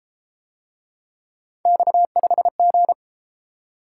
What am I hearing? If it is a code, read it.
Morse code, heard as a keyed tone, about 32 wpm, 700 Hz. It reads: X5G